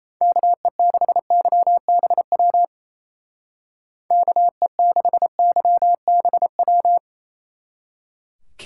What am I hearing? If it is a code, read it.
Morse code, heard as a keyed tone, about 33 wpm, 700 Hz. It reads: KE6YBW KE6YBW
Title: Speed Racing - USA 2x3 Call Signs 28wpm